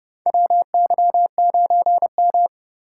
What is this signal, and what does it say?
Morse code, heard as a keyed tone, about 32 wpm, 700 Hz. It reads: WY9M
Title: USA Call Signs - 4 Characters - 3x Character Spacing - 30wpm